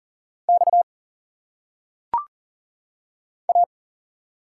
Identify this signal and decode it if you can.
Morse code, heard as a keyed tone, about 40 wpm, 700 Hz. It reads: X A
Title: Single Letters - Phonetic Pronunciation 40wpm